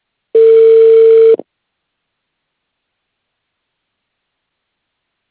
Ringback tone (China, 450 Hz).